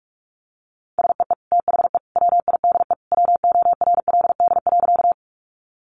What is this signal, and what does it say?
Morse code, heard as a keyed tone, about 44 wpm, 700 Hz. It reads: SEE THE WIDE WORLD.
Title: Sentences from Top 900 Words 45wpm